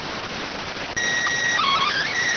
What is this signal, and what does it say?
SSTV (slow-scan television), PD 290.